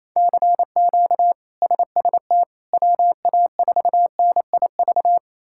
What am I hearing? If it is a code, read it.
Morse code, heard as a keyed tone, about 28 wpm, 700 Hz. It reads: CQ SST WA4NIV
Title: K1USN SST QSO Practice - 28wpm